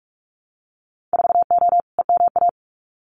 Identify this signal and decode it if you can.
Morse code, heard as a keyed tone, about 44 wpm, 700 Hz. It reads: 4O EMA
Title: Winter Field Day Exchanges Encore 45wpm